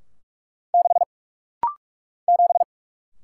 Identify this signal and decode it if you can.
Morse code, heard as a keyed tone, about 44 wpm, 700 Hz. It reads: 6 7
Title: Single Letter-Number - Rapid-Fire 45wpm